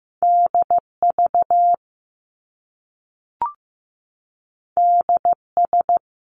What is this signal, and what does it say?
Morse code, heard as a keyed tone, about 15 wpm, 700 Hz. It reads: DV DS
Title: Top Two Letter Combinations in Words 15wpm